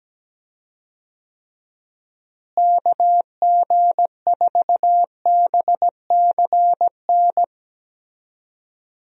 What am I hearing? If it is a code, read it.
Morse code, heard as a keyed tone, about 17 wpm, 700 Hz. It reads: KG4BCN
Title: USA Call Signs - 6 Characters - Repeated 3x - 17wpm